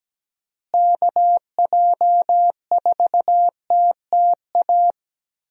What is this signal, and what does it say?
Morse code, heard as a keyed tone, about 17 wpm, 700 Hz. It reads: KJ4TTA
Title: USA Call Signs - 6 Characters - Repeated 3x - 17wpm